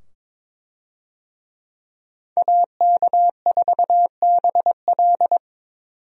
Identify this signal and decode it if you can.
Morse code, heard as a keyed tone, about 22 wpm, 700 Hz. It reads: AK4BL
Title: USA Call Signs - 5 Characters - 22wpm